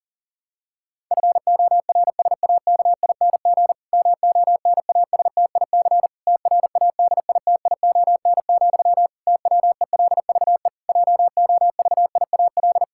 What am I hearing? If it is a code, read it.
Morse code, heard as a keyed tone, about 40 wpm, 700 Hz. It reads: FORSAKING MONASTIC TRADITION, TWELVE JOVIAL
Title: Pangram Challenge v2 - 40wpm